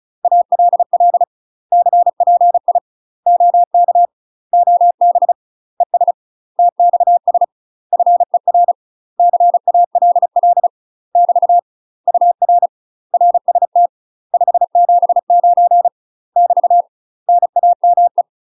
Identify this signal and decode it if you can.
Morse code, heard as a keyed tone, about 34 wpm, 700 Hz. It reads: ALL CPI OK OB ES TXS FER CALL <BT> UR RST 579 <BT> NAME
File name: UK MASTER QSOs TRAINING / 35WPM / UK MASTER QSOs 0004